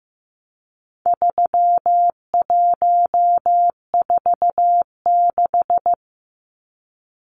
Morse code, heard as a keyed tone, about 15 wpm, 700 Hz. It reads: 3146
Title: Numbers - 4-Digit 15wpm